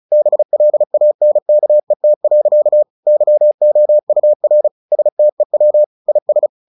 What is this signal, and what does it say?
Morse code, heard as a keyed tone, about 34 wpm, 600 Hz. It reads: BLANKET. YOUR STEW IS
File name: the old man and the sea / the old man and the sea 1 35wpm